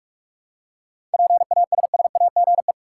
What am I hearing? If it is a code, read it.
Morse code, heard as a keyed tone, about 47 wpm, 700 Hz. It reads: PASSAGE
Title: Seven Letter Words - 45wpm